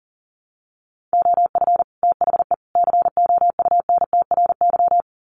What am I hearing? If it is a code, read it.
Morse code, heard as a keyed tone, about 40 wpm, 700 Hz. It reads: OF THE COUNTRY
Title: Sets of 3 Words 40wpm